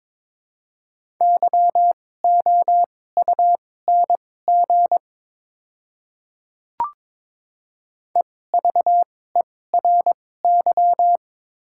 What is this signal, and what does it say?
Morse code, heard as a keyed tone, about 22 wpm, spaced as for 16 wpm, 700 Hz. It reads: YOUNG EVERY
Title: 2x Character Spacing - Top 500 Words - 22wpm